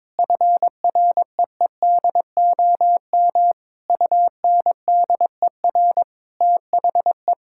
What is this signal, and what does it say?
Morse code, heard as a keyed tone, about 21 wpm, 700 Hz. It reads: FREEDOM UNDER THE
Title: Sentences from Top 1000 Words - Review Part 5 - 22wpm